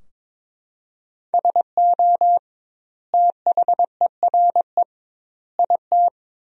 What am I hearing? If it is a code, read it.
Morse code, heard as a keyed tone, about 22 wpm, 700 Hz. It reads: SO THERE IT
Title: 2x Word-Spacing - Sentences from Top 100 Words - 22wpm